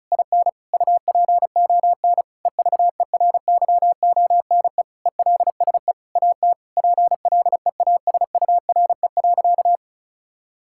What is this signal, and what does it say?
Morse code, heard as a keyed tone, about 34 wpm, 700 Hz. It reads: IN UPON EVERYONE ELSE AT PLEASURE.
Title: Sentences from Top 1600 Words 35wpm